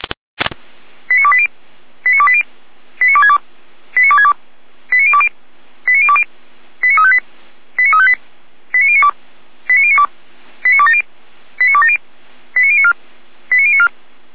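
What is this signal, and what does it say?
Five-tone selective call (ZVEI), heard as quick runs of five tones, tones of about 70 ms. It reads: 89280 89280 89262 89262 89020 89020 89358 89358 89092 89092 89270 89270 89094 89094
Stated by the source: ZVEI five-tone call received by SDR.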